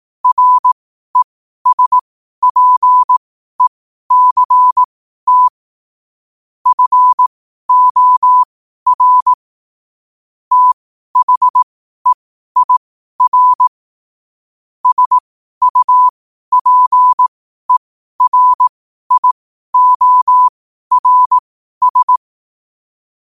Morse code, heard as a keyed tone, about 18 wpm, spaced as for 12 wpm, 1000 Hz. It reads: RESPECT FOR THEIR SUPERIORS
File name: qotdfiles / 12wpm